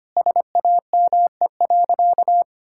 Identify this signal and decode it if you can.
Morse code, heard as a keyed tone, about 25 wpm, 700 Hz. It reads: SAME.
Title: Sentences from Top 1000 Words - Review Part 6 - 25wpm